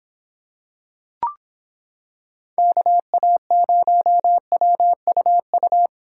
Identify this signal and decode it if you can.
Morse code, heard as a keyed tone, about 26 wpm, 700 Hz. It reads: KA0WUU